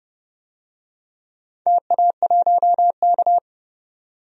Morse code, heard as a keyed tone, about 30 wpm, 700 Hz. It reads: TA1K
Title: DXCC-100 Call Signs - 4 Characters - 30wpm